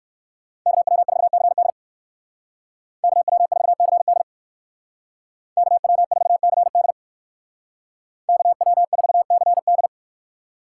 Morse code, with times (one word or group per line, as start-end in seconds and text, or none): 0.7-1.7 KW4CD
3.0-4.2 KW4CD
5.6-6.9 KW4CD
8.3-9.9 KW4CD